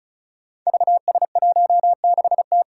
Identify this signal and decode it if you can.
Morse code, heard as a keyed tone, about 36 wpm, 700 Hz. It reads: VS16T
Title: Five Challenging Letter-Number Combinations 35wpm